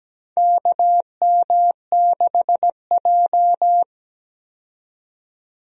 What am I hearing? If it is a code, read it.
Morse code, heard as a keyed tone, about 17 wpm, 700 Hz. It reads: KM6J
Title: USA Call Signs - 4 Characters - 17wpm